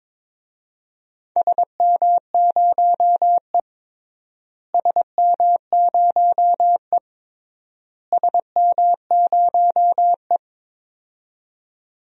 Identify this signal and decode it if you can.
Morse code, heard as a keyed tone, about 22 wpm, 700 Hz. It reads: SM0E SM0E SM0E
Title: DXCC-100 Call Signs - 4 Characters - Repeated 3x - 22wpm